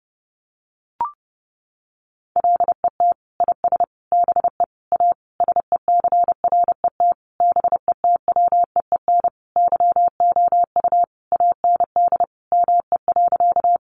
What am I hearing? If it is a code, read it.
Morse code, heard as a keyed tone, about 30 wpm, 700 Hz. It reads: LET IS BE A SECRET BETWEEN YOU AND ME.
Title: Sentences from Top 1200 Words 30wpm